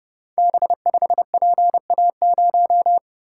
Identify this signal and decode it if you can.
Morse code, heard as a keyed tone, about 29 wpm, 700 Hz. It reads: B5PA0
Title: Code Groups - 5 Letter:Numbers - 30wpm